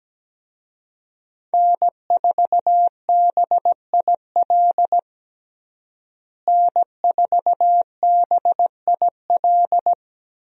Morse code, heard as a keyed tone, about 17 wpm, 700 Hz. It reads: N4BIL N4BIL
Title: USA Call Signs - 5 Characters - Repeated 2x - 17wpm